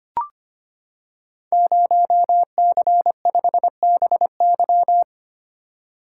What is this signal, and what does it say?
Morse code, heard as a keyed tone, about 25 wpm, 700 Hz. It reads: 0C5BY
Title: Code Groups - 5 Letter:Numbers - 25wpm